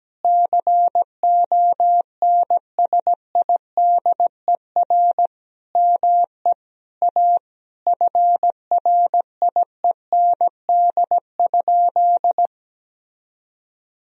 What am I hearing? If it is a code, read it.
Morse code, heard as a keyed tone, about 17 wpm, 700 Hz. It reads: CONSIDER ME A FRIEND?
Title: Sentences from Top 1000 Words - Review Part 4 - 17wpm